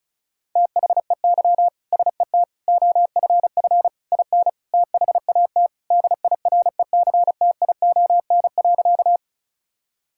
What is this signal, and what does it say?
Morse code, heard as a keyed tone, about 34 wpm, 700 Hz. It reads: THEY SET OFF IN THAT DIRECTION.
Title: Sentences from Top 1000 Words - Review Part 5 - 35wpm